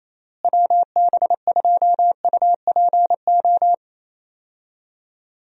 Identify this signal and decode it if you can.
Morse code, heard as a keyed tone, about 27 wpm, 700 Hz. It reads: WB2UPO
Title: USA Call Signs - 6 Characters - Repeated 3x - 28wpm